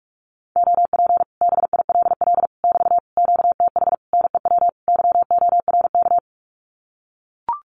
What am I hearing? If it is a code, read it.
Morse code, heard as a keyed tone, about 44 wpm, 700 Hz. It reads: OP BILL <BT> QTH NEW YORK
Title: Common QSO Phrases Encore 45wpm